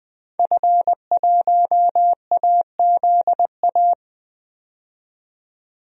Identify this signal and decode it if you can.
Morse code, heard as a keyed tone, about 20 wpm, 700 Hz. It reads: F1AZA
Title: DXCC-100 Call Signs - 5 Characters - 20wpm